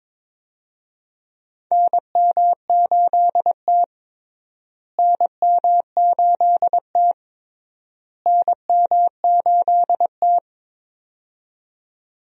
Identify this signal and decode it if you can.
Morse code, heard as a keyed tone, about 22 wpm, 700 Hz. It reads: NM8T NM8T NM8T
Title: USA Call Signs - 4 Characters - Repeated 3x - 22wpm